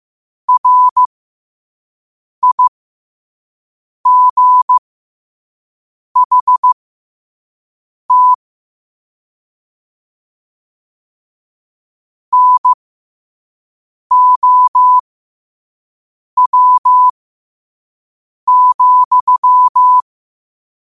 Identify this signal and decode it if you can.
Morse code, heard as a keyed tone, about 15 wpm, spaced as for 5 wpm, 1000 Hz. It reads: RIGHT NOW,
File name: qotdfiles / 5wpm